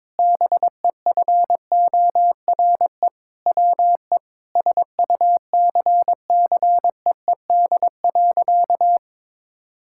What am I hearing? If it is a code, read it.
Morse code, heard as a keyed tone, about 22 wpm, 700 Hz. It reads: BEFORE WE SUCCEED.
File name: Sentences with 7-Word Phrases - 22wpm